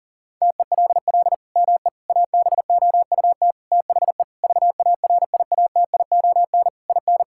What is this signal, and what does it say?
Morse code, heard as a keyed tone, about 37 wpm, 700 Hz. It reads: TELL ME ABOUT THE VARIATION IN
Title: Sentences from Top 1600 Words 40wpm